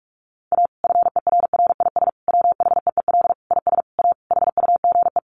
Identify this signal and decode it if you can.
Morse code, heard as a keyed tone, about 44 wpm, 700 Hz. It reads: A FERRIS WHEEL IS A HUGE
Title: Additional Easy Phrases 45wpm